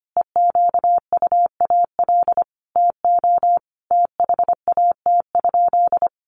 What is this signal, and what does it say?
Morse code, heard as a keyed tone, about 25 wpm, 700 Hz. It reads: EQUAL TO THAT?
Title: Sentences from Top 1400 Words 25wpm